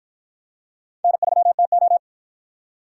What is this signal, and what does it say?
Morse code, heard as a keyed tone, about 53 wpm, 700 Hz. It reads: N3TO